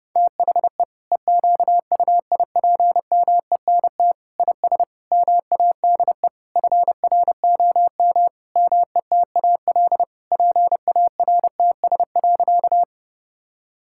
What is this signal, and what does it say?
Morse code, heard as a keyed tone, about 32 wpm, 700 Hz. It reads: THE EQUIPMENT IS MADE FROM METAL PARTS.
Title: Sentences from Top 1000 Words - Review Part 6 - 30wpm